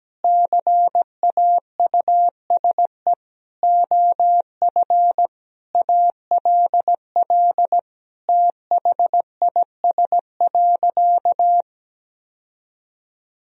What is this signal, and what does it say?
Morse code, heard as a keyed tone, about 17 wpm, 700 Hz. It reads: CAUSE OF ALL THIS.
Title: Sentences from Top 1000 Words - Review Part 1 - 17wpm